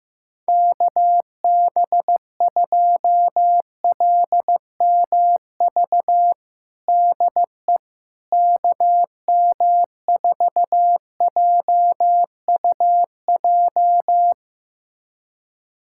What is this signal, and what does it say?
Morse code, heard as a keyed tone, about 15 wpm, 700 Hz. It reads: KB2LMV DE KM4JUJ
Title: USA Callsign DE Callsign - 6 Characters - 15wpm